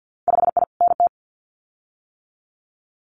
Morse code, heard as a keyed tone, about 50 wpm, 700 Hz. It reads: 5I NT
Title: Winter Field Day Exchanges Encore 50wpm